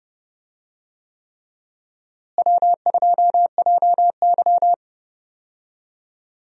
Morse code, heard as a keyed tone, about 31 wpm, 700 Hz. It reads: W2JY
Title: USA Call Signs - 4 Characters - Repeated 3x - 30wpm